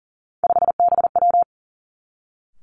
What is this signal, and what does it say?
Morse code, heard as a keyed tone, about 41 wpm, 700 Hz. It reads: 5BW
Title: Three Challenging Letter-Number Combinations 40wpm